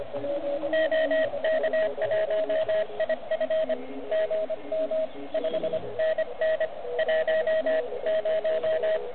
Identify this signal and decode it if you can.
Morse code, heard as a keyed tone, about 25 wpm, 630 Hz, with radio noise. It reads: OK1IF GM 5NN 10
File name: ARCHIV / OK QRP 2021 / REC05